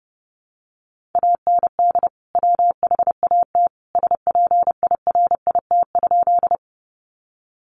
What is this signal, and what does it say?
Morse code, heard as a keyed tone, about 30 wpm, 700 Hz. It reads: AND WHAT SPIRIT?
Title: Sentences from Top 1400 Words 30wpm